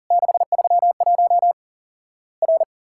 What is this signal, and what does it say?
Morse code, heard as a keyed tone, about 42 wpm, 700 Hz. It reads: 631 R
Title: Worked All Europe DX Contest - QTC Exchange - 40wpm